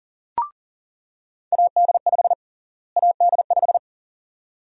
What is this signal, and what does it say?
Morse code, heard as a keyed tone, about 40 wpm, 700 Hz. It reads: AD5 AD5
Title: USA Call Sign Prefixes - Repeated 2x - 40wpm